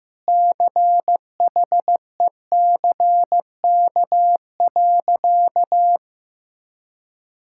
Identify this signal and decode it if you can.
Morse code, heard as a keyed tone, about 16 wpm, 700 Hz. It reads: CHECK.
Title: Sentences from Top 1000 Words - Review Part 6 - 15wpm